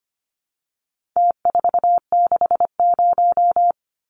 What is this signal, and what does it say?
Morse code, heard as a keyed tone, about 25 wpm, 700 Hz. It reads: T460
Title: Numbers - Contest Style - 4-Digit 25wpm